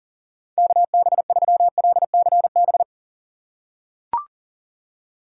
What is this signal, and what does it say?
Morse code, heard as a keyed tone, about 38 wpm, 700 Hz. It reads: KB3LCB
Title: USA Call Signs - 6 Characters - 4x Character Spacing - 40wpm